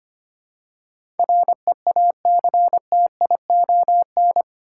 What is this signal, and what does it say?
Morse code, heard as a keyed tone, about 25 wpm, 700 Hz. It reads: REACTION
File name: 8 Letter Words - First 3 Letters Given - 25wpm